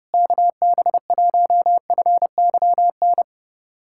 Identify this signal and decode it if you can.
Morse code, heard as a keyed tone, about 30 wpm, 700 Hz. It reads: KB1FYN
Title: USA Callsign DE Callsign - 6 Characters - 30wpm